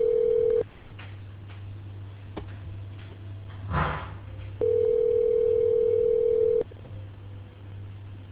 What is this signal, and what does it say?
Ringback tone (North America), 440 + 480 Hz.